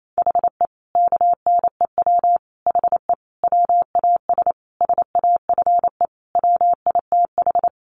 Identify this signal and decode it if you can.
Morse code, heard as a keyed tone, about 28 wpm, 700 Hz. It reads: HE KNEW HE WAS SAFE WITH
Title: Sentences from Top 900 Words 28wpm